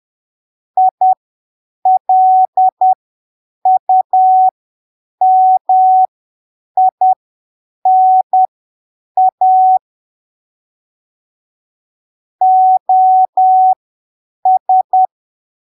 Morse code, heard as a keyed tone, about 10 wpm, spaced as for 7 wpm, 750 Hz. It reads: ILUMINA OS
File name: arquivos / texto / texto 29 10ppm